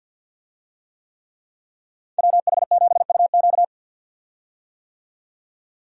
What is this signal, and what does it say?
Morse code, heard as a keyed tone, about 51 wpm, 700 Hz. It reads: WH7UQ
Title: USA Call Signs - 5 Characters - 50wpm